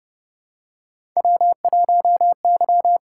Morse code, heard as a keyed tone, about 28 wpm, 700 Hz. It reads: W1Y
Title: USA Call Signs - 4 Characters - 6x Character Spacing - 30wpm